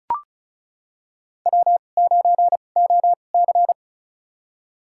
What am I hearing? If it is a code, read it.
Morse code, heard as a keyed tone, about 35 wpm, spaced as for 25 wpm, 700 Hz. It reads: W9OC
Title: USA Call Signs - 4 Characters - 2x Character Spacing - 35wpm